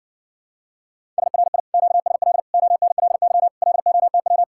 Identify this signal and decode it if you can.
Morse code, heard as a keyed tone, about 59 wpm, 700 Hz. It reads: SRI QSL ONLY LOTW